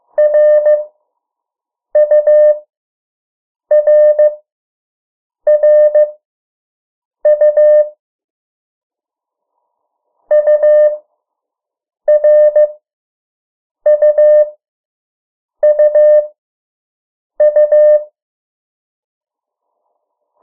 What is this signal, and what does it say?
Morse code, heard as a keyed tone, about 15 wpm, spaced as for 6 wpm, 600 Hz. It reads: RURRU URUUU
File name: morse mp3 / 03 urok 2